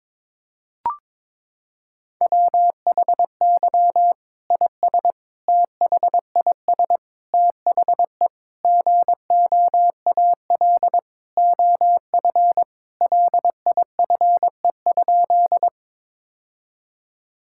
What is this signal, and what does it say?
Morse code, heard as a keyed tone, about 22 wpm, 700 Hz. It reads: WHY IS THIS THE GOAL OF LIFE?
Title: Sentences from Top 1000 Words - Review Part 5 - 22wpm